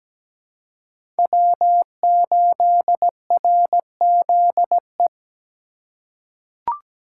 Morse code, heard as a keyed tone, about 17 wpm, 700 Hz. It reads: W8RZE
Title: USA Call Signs - 5 Characters - 2x Character Spacing - 17wpm